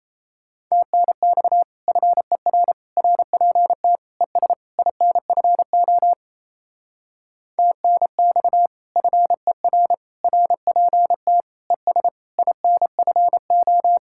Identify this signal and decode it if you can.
Morse code, heard as a keyed tone, about 36 wpm, 700 Hz. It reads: TNX FER RPT ES INFO TNX FER RPT ES INFO